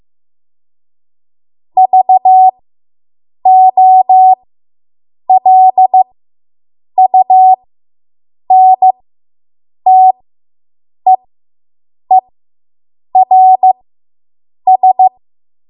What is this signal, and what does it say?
Morse code, heard as a keyed tone, about 15 wpm, spaced as for 7 wpm, 750 Hz. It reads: VOLUNTEERS